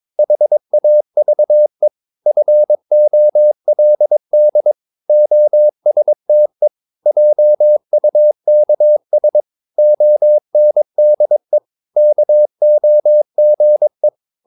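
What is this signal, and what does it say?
Morse code, heard as a keyed tone, about 22 wpm, 600 Hz. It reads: HAVE FOLD OSTE JUKS ONDE KOGE
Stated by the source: Morse practice of short Danish words at 22 wpm, from CW med Gnister.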